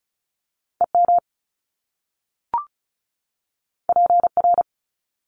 Morse code, heard as a keyed tone, about 36 wpm, 700 Hz. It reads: EM PR
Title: Top Two Letter Combinations in Words 35wpm